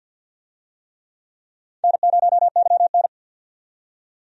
Morse code, heard as a keyed tone, about 47 wpm, 700 Hz. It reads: N0YN
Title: USA Call Signs - 4 Characters - 50wpm